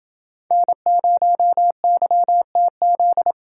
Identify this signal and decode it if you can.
Morse code, heard as a keyed tone, about 26 wpm, 700 Hz. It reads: N0YTZ